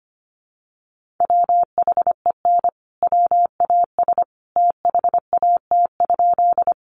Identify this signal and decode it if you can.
Morse code, heard as a keyed tone, about 25 wpm, 700 Hz. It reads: WHEN WAS THAT?
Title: Sentences from Top 100 Words 25wpm